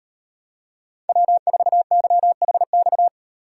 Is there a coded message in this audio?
Yes: Morse code from 1.1 to 3.1 s, heard as a keyed tone, about 38 wpm, 700 Hz. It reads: W4YHX